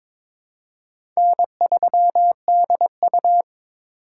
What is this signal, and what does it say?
Morse code, heard as a keyed tone, about 22 wpm, 700 Hz. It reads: N3DU
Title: USA Call Signs - 4 Characters - 4x Character Spacing - 22wpm